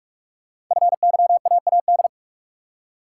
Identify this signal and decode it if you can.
Morse code, heard as a keyed tone, about 44 wpm, 700 Hz. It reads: FYAAD
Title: Code Groups - 5 Letters - 45wpm